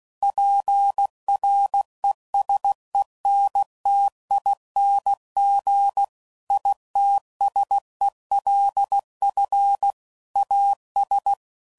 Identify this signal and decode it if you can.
Morse code, heard as a keyed tone, about 16 wpm, 800 Hz. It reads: PRESENTING ITSELF AS
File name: CW Code Groups @15WPM / Technical Subjects / XC